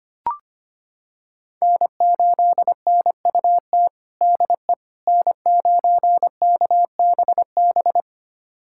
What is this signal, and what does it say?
Morse code, heard as a keyed tone, about 25 wpm, 700 Hz. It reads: N8NUT DE N9KBB
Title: USA Callsign DE Callsign - 5 Characters - 25wpm